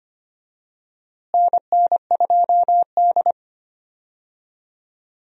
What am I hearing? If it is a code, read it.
Morse code, heard as a keyed tone, about 25 wpm, 700 Hz. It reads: NN2D
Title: USA Call Signs - 4 Characters - Repeated 3x - 25wpm